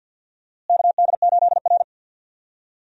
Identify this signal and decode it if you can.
Morse code, heard as a keyed tone, about 52 wpm, 700 Hz. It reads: KD8R
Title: USA Call Signs - 4 Characters - Repeated 3x - 50wpm